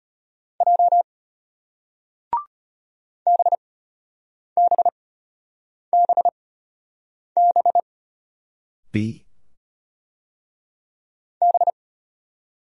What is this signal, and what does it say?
Morse code, heard as a keyed tone, about 33 wpm, spaced as for 8 wpm, 700 Hz. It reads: J BBBB B